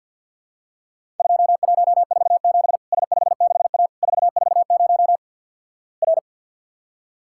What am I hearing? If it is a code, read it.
Morse code, heard as a keyed tone, about 48 wpm, 700 Hz. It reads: 2147 S56A 440 R
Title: Worked All Europe DX Contest - QTC Exchange with Repeats - 50wpm